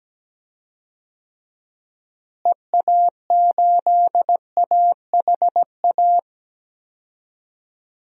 Morse code, heard as a keyed tone, about 17 wpm, 700 Hz. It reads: EA8AHA
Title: DXCC-100 Call Signs - 6 Characters - Repeated 2x - 17wpm